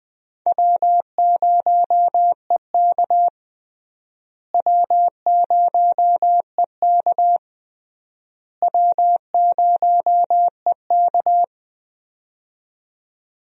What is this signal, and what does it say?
Morse code, heard as a keyed tone, about 20 wpm, 700 Hz. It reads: W0EK W0EK W0EK
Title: USA Call Signs - 4 Characters - Repeated 3x - 20wpm